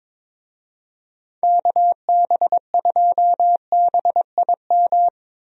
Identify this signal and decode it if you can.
Morse code, heard as a keyed tone, about 22 wpm, 700 Hz. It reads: KB2BIM
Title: USA Call Signs - 6 Characters - Repeated 2x - 22wpm